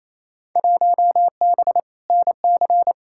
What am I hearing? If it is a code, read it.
Morse code, heard as a keyed tone, about 28 wpm, 700 Hz. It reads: 1B NC
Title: Field Day Exchanges Encore - 28wpm